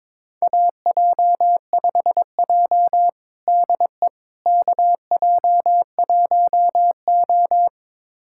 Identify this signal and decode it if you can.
Morse code, heard as a keyed tone, about 22 wpm, 700 Hz. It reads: AJ5J DE KJ1O
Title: USA Callsign DE Callsign - 4 Characters - 22wpm